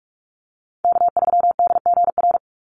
Morse code, heard as a keyed tone, about 44 wpm, 700 Hz. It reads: K3DGR
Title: Call Signs - 45wpm